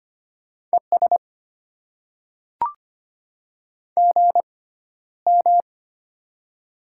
Morse code, heard as a keyed tone, about 25 wpm, spaced as for 13 wpm, 700 Hz. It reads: ES GM
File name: Top 20 QSO Elements - 6x Character Spacing - 25wpm